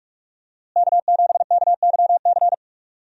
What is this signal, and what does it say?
Morse code, heard as a keyed tone, about 42 wpm, 700 Hz. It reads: K7KYC
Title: USA Call Signs - 5 Characters - Repeated 2x - 45wpm